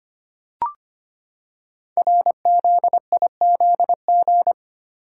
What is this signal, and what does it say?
Morse code, heard as a keyed tone, about 25 wpm, 700 Hz. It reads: RZIZG
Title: Code Groups - 5 Letters - 25wpm